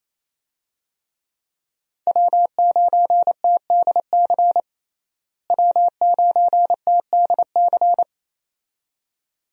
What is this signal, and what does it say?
Morse code, heard as a keyed tone, about 28 wpm, 700 Hz. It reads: W9TDC W9TDC
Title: USA Call Signs - 5 Characters - Repeated 2x - 28wpm